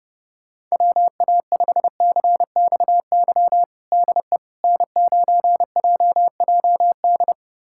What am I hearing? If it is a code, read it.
Morse code, heard as a keyed tone, about 30 wpm, 700 Hz. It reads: WA5CXY DE N9JJD
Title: USA Callsign DE Callsign 30wpm